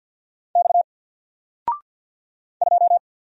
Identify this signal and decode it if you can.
Morse code, heard as a keyed tone, about 50 wpm, 700 Hz. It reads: X 2
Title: Single Letter-Number 50wpm